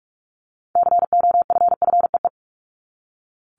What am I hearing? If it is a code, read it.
Morse code, heard as a keyed tone, about 44 wpm, 700 Hz. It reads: COFFEE
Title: Top 1751-2000 Words 45wpm